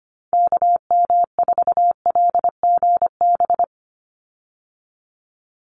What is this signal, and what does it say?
Morse code, heard as a keyed tone, about 25 wpm, 690 Hz. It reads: KM4LGB
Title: USA Call Signs 25wpm